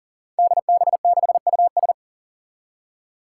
Morse code, heard as a keyed tone, about 40 wpm, 700 Hz. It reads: DB6US
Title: Five Challenging Letter-Number Combinations 40wpm